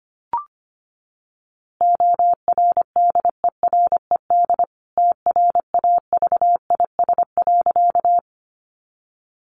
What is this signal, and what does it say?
Morse code, heard as a keyed tone, about 25 wpm, 700 Hz. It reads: ORDERED TRAVIS.
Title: A Sound Of Thunder 25wpm